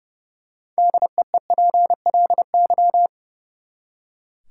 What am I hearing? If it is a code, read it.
Morse code, heard as a keyed tone, about 30 wpm, 700 Hz. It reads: DEEPLY
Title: Six Letter Words - 30wpm